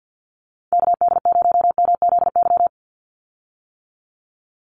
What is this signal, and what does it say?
Morse code, heard as a keyed tone, about 52 wpm, 700 Hz. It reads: KD0MZY
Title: USA Call Signs 50wpm